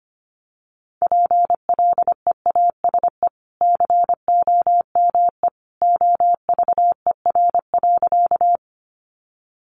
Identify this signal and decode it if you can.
Morse code, heard as a keyed tone, about 25 wpm, 700 Hz. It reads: PLEASE COME OVER.
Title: Sentences from Top 1500 Words 25wpm